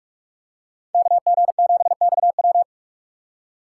Morse code, heard as a keyed tone, about 46 wpm, 690 Hz. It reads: KG7XW
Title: USA Call Signs - 5 Characters - 4x Character Spacing - 45wpm